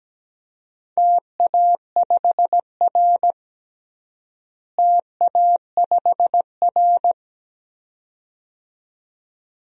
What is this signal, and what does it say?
Morse code, heard as a keyed tone, about 17 wpm, 700 Hz. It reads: TA5R TA5R
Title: DXCC-100 Call Signs - 4 Characters - Repeated 2x - 17wpm